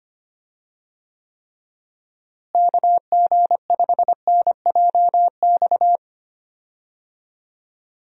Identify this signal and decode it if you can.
Morse code, heard as a keyed tone, about 25 wpm, 700 Hz. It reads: KG5NJX
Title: USA Call Signs - 6 Characters - Repeated 2x - 25wpm